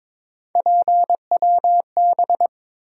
Morse code, heard as a keyed tone, about 22 wpm, 700 Hz. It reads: PWB
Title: USA Call Sign Postfixes - 3 Letters 22wpm